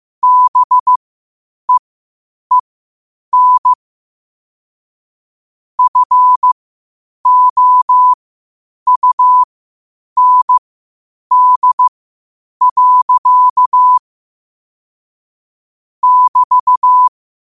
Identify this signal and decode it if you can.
Morse code, heard as a keyed tone, about 15 wpm, spaced as for 8 wpm, 1000 Hz. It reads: BEEN FOUND. <BT>
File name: qotdfiles / 8wpm